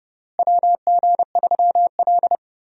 Morse code, heard as a keyed tone, about 30 wpm, 700 Hz. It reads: WG3L
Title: USA Call Signs - 4 Characters - Repeated 3x - 30wpm